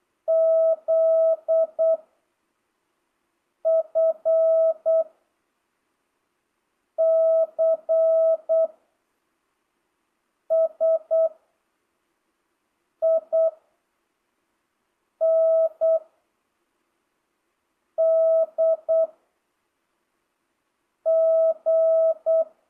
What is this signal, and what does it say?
Morse code, heard as a keyed tone, about 8 wpm, 660 Hz. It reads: Z F C S I N D G
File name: jh8cbh / 2025cw